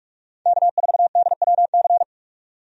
Morse code, heard as a keyed tone, about 47 wpm, 700 Hz. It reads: K4DWC
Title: USA Call Signs - 5 Characters - 45wpm